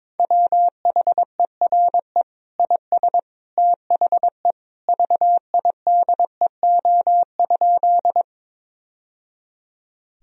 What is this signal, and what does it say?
Morse code, heard as a keyed tone, about 22 wpm, 700 Hz. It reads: WHERE IS THE VIDEO?
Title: Sentences from Top 1000 Words - Review Part 4 - 22wpm